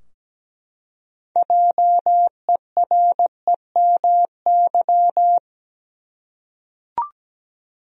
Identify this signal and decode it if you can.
Morse code, heard as a keyed tone, about 17 wpm, 700 Hz. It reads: JEREMY
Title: USA Common Names - 17wpm